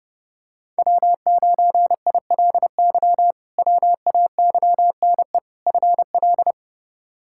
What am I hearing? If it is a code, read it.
Morse code, heard as a keyed tone, about 30 wpm, 700 Hz. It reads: W9ILY WAYNE FL
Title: K1USN SST QSO Practice - 30wpm